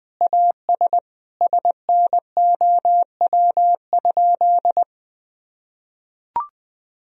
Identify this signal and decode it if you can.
Morse code, heard as a keyed tone, about 21 wpm, 700 Hz. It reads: AS SNOW?
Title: Sentences from Top 1000 Words - Review Part 3 - 20wpm